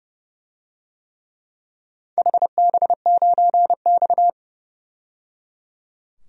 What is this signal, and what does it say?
Morse code, heard as a keyed tone, about 30 wpm, 700 Hz. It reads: HB9X
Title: DXCC-100 Call Signs - 4 Characters - 30wpm